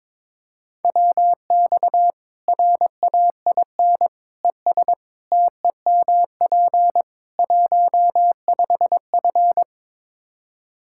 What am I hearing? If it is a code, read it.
Morse code, heard as a keyed tone, about 22 wpm, 700 Hz. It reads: WX RAIN ES TEMP 15F
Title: Practice QSO - Exchange 2 - Info - 22wpm